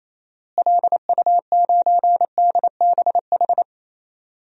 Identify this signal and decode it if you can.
Morse code, heard as a keyed tone, about 27 wpm, 700 Hz. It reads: LU9DBH
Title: DXCC-100 Call Signs - 6 Characters - Repeated 2x - 28wpm